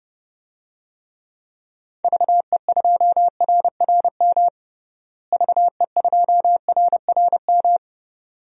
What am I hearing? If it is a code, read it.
Morse code, heard as a keyed tone, about 30 wpm, 700 Hz. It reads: VE2RRM VE2RRM
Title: DXCC-100 Call Signs - 6 Characters - Repeated 2x - 30wpm